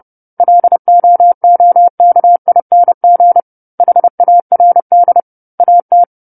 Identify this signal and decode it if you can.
Morse code, heard as a keyed tone, about 30 wpm, 700 Hz. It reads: LOOKING HARD AT
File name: 10-Chapter-10 30 WPM